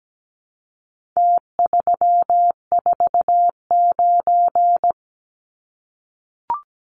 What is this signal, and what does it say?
Morse code, heard as a keyed tone, about 17 wpm, 700 Hz. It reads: T349
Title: Numbers - Contest Style - 4-Digit 17wpm